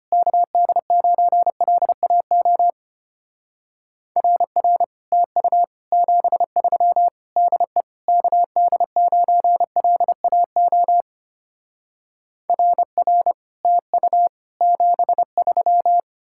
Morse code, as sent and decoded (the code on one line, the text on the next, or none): -.- -.. ----. .-.. .- --- / .-. .-. / - ..- / --... ...-- / -.. / -.- -.. ----. .-.. .- --- / .-. .-. / - ..- / --... ...--
KD9LAO RR TU 73 DE KD9LAO RR TU 73